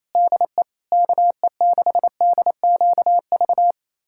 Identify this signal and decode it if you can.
Morse code, heard as a keyed tone, about 27 wpm, 700 Hz. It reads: DE KE6DQV
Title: USA Callsign DE Callsign - 6 Characters - 28wpm